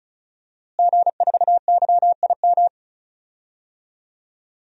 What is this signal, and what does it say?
Morse code, heard as a keyed tone, about 36 wpm, 700 Hz. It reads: G4YIM
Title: DXCC-100 Call Signs - 5 Characters - Repeated 3x - 35wpm